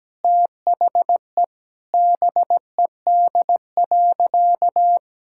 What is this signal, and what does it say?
Morse code, heard as a keyed tone, about 17 wpm, 700 Hz. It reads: THE BED.
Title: Sentences from Top 1600 Words 17wpm